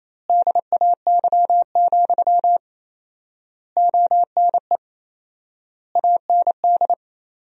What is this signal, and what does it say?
Morse code, heard as a keyed tone, about 28 wpm, 700 Hz. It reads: DAY, ONE AND
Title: 4x Word-Spacing - Sentences from Top 200 Words - 28wpm